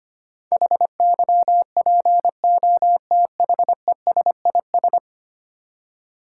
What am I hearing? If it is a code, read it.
Morse code, heard as a keyed tone, about 25 wpm, 700 Hz. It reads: HYPOTHESIS
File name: Ten Letter Words - 25wpm